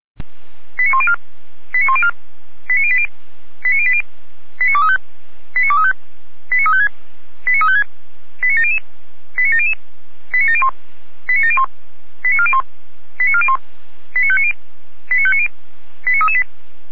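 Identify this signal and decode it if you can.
Five-tone selective call (ZVEI), heard as quick runs of five tones, tones of about 70 ms. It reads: 89104 89104 89080 89080 89235 89235 89356 89356 89700 89700 89701 89701 89401 89401 89590 89590 89338
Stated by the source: Five-tone ZVEI selective call received on an SDR.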